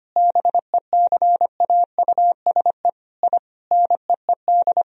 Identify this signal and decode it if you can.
Morse code, heard as a keyed tone, about 25 wpm, 700 Hz. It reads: BECAUSE I NEED
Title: Sentences from Top 1000 Words - Review Part 3 - 25wpm